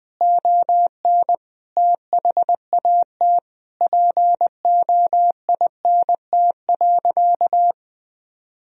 Morse code, heard as a keyed tone, about 20 wpm, 700 Hz. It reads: ON THAT POINT.
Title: Sentences from Top 1000 Words - Review Part 1 - 20wpm